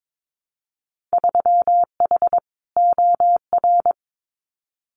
Morse code, heard as a keyed tone, about 22 wpm, 700 Hz. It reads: 3H OR
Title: Winter Field Day Exchanges Encore 22wpm